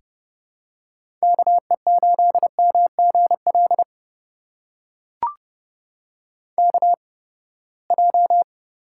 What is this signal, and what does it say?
Morse code, heard as a keyed tone, about 30 wpm, spaced as for 21 wpm, 700 Hz. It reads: KE8MGL KJ
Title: USA Call Signs - 6 Characters - 8x Character Spacing - 30wpm